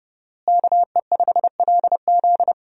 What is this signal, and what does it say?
Morse code, heard as a keyed tone, about 30 wpm, 700 Hz. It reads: KE5LZ
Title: USA Call Signs - 5 Characters - Repeated 3x - 30wpm